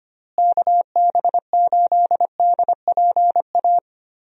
Morse code, heard as a keyed tone, about 25 wpm, 700 Hz. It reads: KB8DPA
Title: USA Call Signs - 6 Characters - Repeated 2x - 25wpm